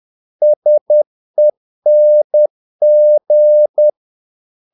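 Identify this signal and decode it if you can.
Morse code, heard as a keyed tone, about 10 wpm, 600 Hz. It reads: SENG
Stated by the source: Morse practice of short Danish words at 10 wpm, from CW med Gnister.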